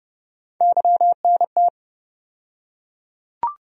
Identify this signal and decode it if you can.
Morse code, heard as a keyed tone, about 30 wpm, 700 Hz. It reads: YNT
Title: USA Call Sign Postfixes - 3 Letters 30wpm